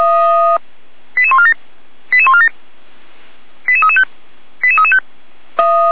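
Five-tone selective call (ZVEI), heard as quick runs of five tones, tones of about 70 ms. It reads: 88147 88147 88335 88335
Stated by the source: ZVEI five-tone call 88397 received on an SDR.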